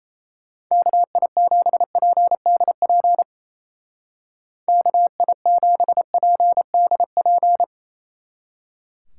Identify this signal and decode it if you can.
Morse code, heard as a keyed tone, about 36 wpm, 700 Hz. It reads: KI7PDP KI7PDP